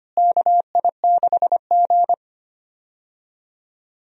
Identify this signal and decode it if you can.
Morse code, heard as a keyed tone, about 25 wpm, 700 Hz. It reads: KI6G
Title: USA Call Signs - 4 Characters - Repeated 3x - 25wpm